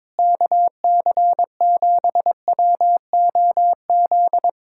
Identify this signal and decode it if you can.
Morse code, heard as a keyed tone, about 22 wpm, 700 Hz. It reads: KC7WOZ
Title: Speed-Racing Common QSO Phrases - 22wpm